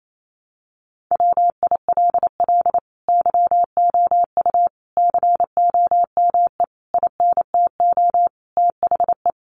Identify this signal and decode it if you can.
Morse code, heard as a keyed tone, about 28 wpm, 700 Hz. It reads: WILL YOU COME INTO THE
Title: Sentences from Top 700 Words 28wpm